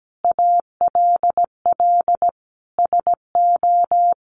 Morse code, heard as a keyed tone, about 17 wpm, 700 Hz. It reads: ALL SO
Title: Sentences from Top 1300 Words 17wpm